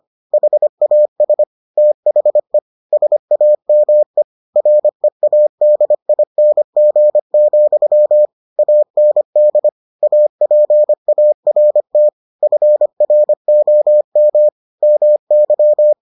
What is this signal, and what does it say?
Morse code, heard as a keyed tone, about 25 wpm, 600 Hz. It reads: HAS THE SAME READING, AND APART FROM MY